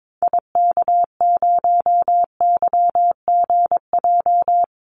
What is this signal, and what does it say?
Morse code, heard as a keyed tone, about 22 wpm, 700 Hz. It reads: IK0YGJ
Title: Call Signs Repeated 2x - 22wpm